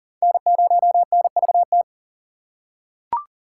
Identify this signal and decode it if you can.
Morse code, heard as a keyed tone, about 40 wpm, 700 Hz. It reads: N0NVT
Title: USA Call Signs - 5 Characters - 2x Character Spacing - 40wpm